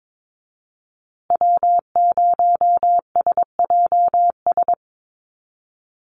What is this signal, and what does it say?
Morse code, heard as a keyed tone, about 22 wpm, 700 Hz. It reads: W0SJS
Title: Call Signs Repeated 2x - 22wpm